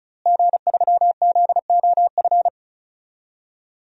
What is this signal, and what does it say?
Morse code, heard as a keyed tone, about 36 wpm, 700 Hz. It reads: G3ZOF
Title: DXCC-100 Call Signs - 5 Characters - Repeated 2x - 35wpm